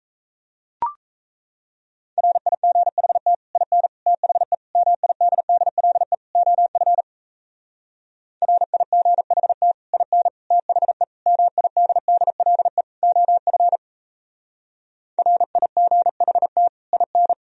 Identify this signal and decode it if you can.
Morse code, heard as a keyed tone, about 35 wpm, 700 Hz. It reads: RIGHT IN THE MIDDLE OF RIGHT IN THE MIDDLE OF RIGHT IN